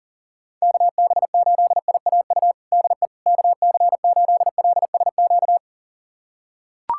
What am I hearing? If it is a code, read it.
Morse code, heard as a keyed tone, about 40 wpm, 700 Hz. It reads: KB8IAU DE KC8LSQ
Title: USA Callsign DE Callsign - 6 Characters - 40wpm